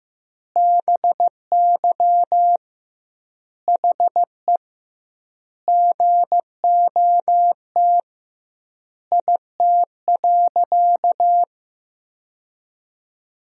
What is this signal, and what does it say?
Morse code, heard as a keyed tone, about 15 wpm, 700 Hz. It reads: BY HE GOT IT.
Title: 2x Word-Spacing - Sentences from Top 300 Words - 15wpm